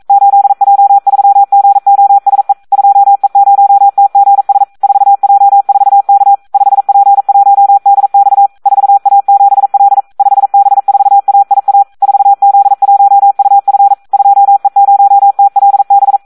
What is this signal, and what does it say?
Morse code, heard as a keyed tone, about 42 wpm, 800 Hz. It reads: 8J3GOSE 2E0TGS 4J4K 5P1DX 4A7L 5B4AIA 4Z1UF 2E0TLB
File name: Audio practica CW / SD 1 42